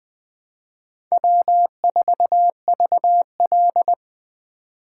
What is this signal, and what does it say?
Morse code, heard as a keyed tone, about 20 wpm, 700 Hz. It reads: W4VL
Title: USA Call Signs - 4x Character Spacing - 20wpm